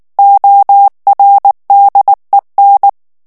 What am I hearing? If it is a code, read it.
Morse code, heard as a keyed tone, about 19 wpm, 800 Hz. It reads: ORDEN
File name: morse mp3 / ESPAGNOL TEXTE EN CLAIR 2 VITESSE 18 MOTS MINUTES